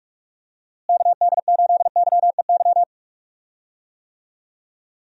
Morse code, heard as a keyed tone, about 46 wpm, 700 Hz. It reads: KD8YEY
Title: USA Call Signs - 6 Characters - Repeated 2x - 45wpm